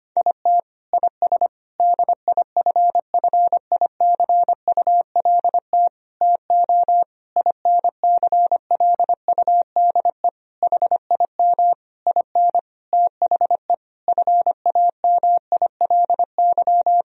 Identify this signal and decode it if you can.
Morse code, heard as a keyed tone, about 25 wpm, 700 Hz. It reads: IT IS DIFFICULT TO INCLUDE HIM IN THE FAMILY
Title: Sentences from Top 1000 Words - Review Part 3 - 25wpm